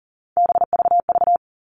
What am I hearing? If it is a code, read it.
Morse code, heard as a keyed tone, about 40 wpm, 700 Hz. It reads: BVV
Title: All Cut Numbers - 3-Digit 40wpm